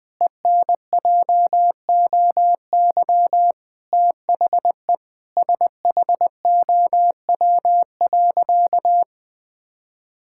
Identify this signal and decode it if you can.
Morse code, heard as a keyed tone, about 21 wpm, 700 Hz. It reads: ENJOY THE SHOW.
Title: Sentences with 4-Word Phrases - 20wpm